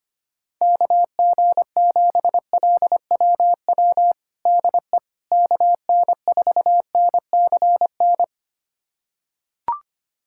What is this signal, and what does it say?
Morse code, heard as a keyed tone, about 25 wpm, 700 Hz. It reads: KG7LWW DE KN4NCN
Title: USA Callsign DE Callsign 25wpm